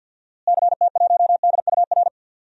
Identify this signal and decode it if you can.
Morse code, heard as a keyed tone, about 52 wpm, 700 Hz. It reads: CT1DUR
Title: DXCC-100 Call Signs - 6 Characters - Repeated 3x - 50wpm